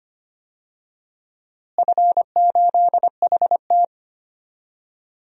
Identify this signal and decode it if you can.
Morse code, heard as a keyed tone, about 25 wpm, 700 Hz. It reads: F8HT
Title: DXCC-100 Call Signs - 4 Characters - Repeated 2x - 25wpm